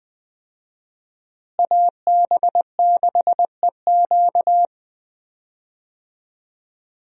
Morse code, heard as a keyed tone, about 21 wpm, 700 Hz. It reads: AB6EQ
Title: USA Call Signs - 5 Characters - Repeated 3x - 20wpm